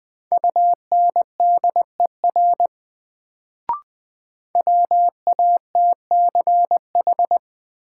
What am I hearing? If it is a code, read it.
Morse code, heard as a keyed tone, about 20 wpm, 700 Hz. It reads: UNDER WATCH
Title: Sentences from Top 300 Words - New Word Drill - 20wpm